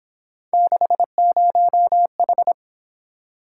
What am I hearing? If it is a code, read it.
Morse code, heard as a keyed tone, about 28 wpm, 700 Hz. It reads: 60H
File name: Speed-Racing - Three Letter-Number Combinations - 22wpm